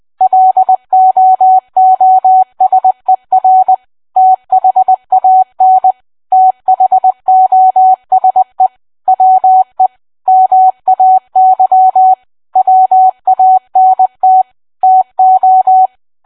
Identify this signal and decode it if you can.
Morse code, heard as a keyed tone, about 20 wpm, 750 Hz. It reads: LOOSER THAN THOSE WE MAY WANT TO